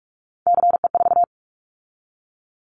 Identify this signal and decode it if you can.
Morse code, heard as a keyed tone, about 46 wpm, 700 Hz. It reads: CE4
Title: Callsign Prefixes for Top 100 DXCC - 45wpm